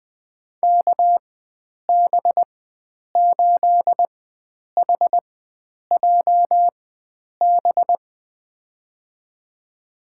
Morse code, heard as a keyed tone, about 21 wpm, spaced as for 9 wpm, 700 Hz. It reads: KB8HJB
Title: USA Call Signs - 6 Characters - 4x Character Spacing - 20wpm